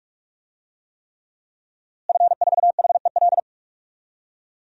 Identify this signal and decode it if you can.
Morse code, heard as a keyed tone, about 41 wpm, 700 Hz. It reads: F4HEL
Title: DXCC-100 Call Signs - 5 Characters - Repeated 3x - 45wpm